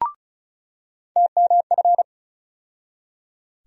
Morse code, heard as a keyed tone, about 34 wpm, 700 Hz. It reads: TMF
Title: USA Call Sign Postfixes - 3 Letters 35wpm